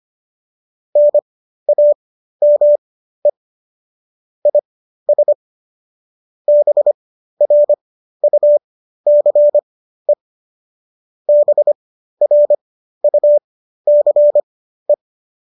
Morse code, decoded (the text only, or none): NAME IS BRUCE BRUCE